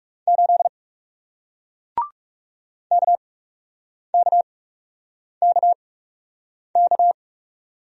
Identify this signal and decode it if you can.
Morse code, heard as a keyed tone, about 41 wpm, spaced as for 8 wpm, 700 Hz. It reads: 8 KKKK